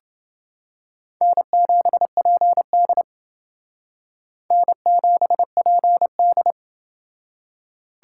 Morse code, heard as a keyed tone, about 28 wpm, 700 Hz. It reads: N7PD N7PD